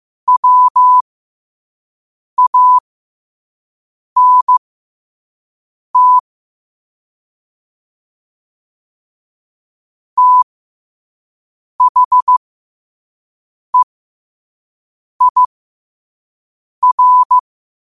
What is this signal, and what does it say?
Morse code, heard as a keyed tone, about 15 wpm, spaced as for 5 wpm, 1000 Hz. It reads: WANT THEIR
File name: qotdfiles / 5wpm